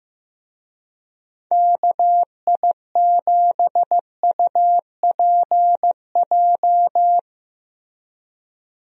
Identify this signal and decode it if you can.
Morse code, heard as a keyed tone, about 16 wpm, 700 Hz. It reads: KI7UPJ
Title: USA Call Signs - 6 Characters - Repeated 2x - 15wpm